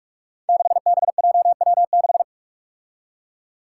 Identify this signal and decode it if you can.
Morse code, heard as a keyed tone, about 43 wpm, 700 Hz. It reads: 6BJW6
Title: Five Challenging Letter-Number Combinations 45wpm